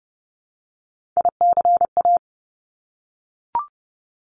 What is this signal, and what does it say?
Morse code, heard as a keyed tone, about 30 wpm, 700 Hz. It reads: ICA
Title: Top Three Letter Combinations in Words 30wpm